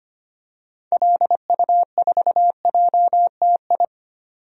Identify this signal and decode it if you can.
Morse code, heard as a keyed tone, about 25 wpm, 700 Hz. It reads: LU4JTI
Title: DXCC-100 Call Signs - 6 Characters - 25wpm